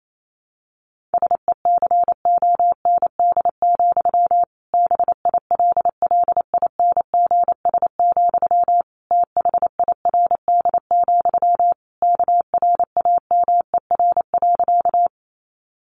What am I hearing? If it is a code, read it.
Morse code, heard as a keyed tone, about 28 wpm, 700 Hz. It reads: SECOND, BILLINGS, THIRD, KRAMER.
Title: A Sound Of Thunder 28wpm